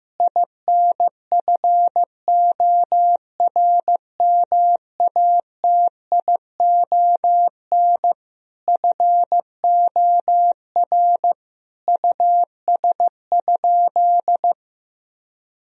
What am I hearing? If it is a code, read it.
Morse code, heard as a keyed tone, about 15 wpm, 700 Hz. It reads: INFORMATION FOR US?
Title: Sentences from Top 1000 Words - Review Part 3 - 15wpm